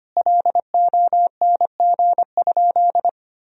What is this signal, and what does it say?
Morse code, heard as a keyed tone, about 25 wpm, 700 Hz. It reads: LONG?
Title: Sentences from Top 1000 Words - Review Part 3 - 25wpm